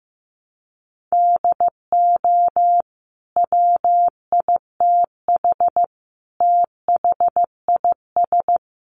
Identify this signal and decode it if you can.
Morse code, heard as a keyed tone, about 15 wpm, 700 Hz. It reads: DO WITH THIS
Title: Sentences from Top 1400 Words 15wpm